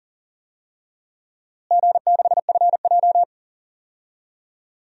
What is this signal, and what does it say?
Morse code, heard as a keyed tone, about 40 wpm, 700 Hz. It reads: G6FJ
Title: DXCC-100 Call Signs - 4 Characters - Repeated 3x - 40wpm